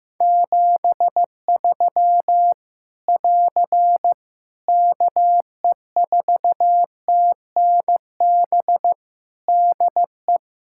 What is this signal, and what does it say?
Morse code, heard as a keyed tone, about 15 wpm, 700 Hz. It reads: 73 <AR> KE4TNB DE
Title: Practice QSO - Exchange 4 - Ending - 15wpm